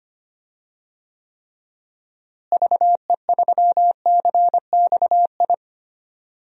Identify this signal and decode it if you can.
Morse code, heard as a keyed tone, about 25 wpm, 700 Hz. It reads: VE3CXI
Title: DXCC-100 Call Signs - 6 Characters - Repeated 3x - 25wpm